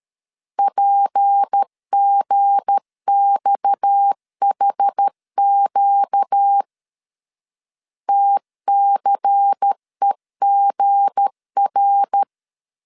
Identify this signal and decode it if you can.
Morse code, heard as a keyed tone, about 13 wpm, 800 Hz. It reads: PGXHQ TCEGR
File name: LEZIONI / 7 GF9Z / 7 G 12wpm 2055